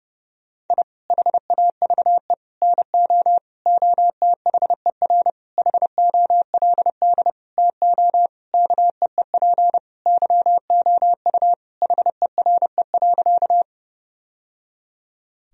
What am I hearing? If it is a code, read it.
Morse code, heard as a keyed tone, about 30 wpm, 700 Hz. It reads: I HAVE NO OTHER HOLD TO KEEP YOU HERE.
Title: Sentences from Top 1000 Words - Review Part 2 - 30wpm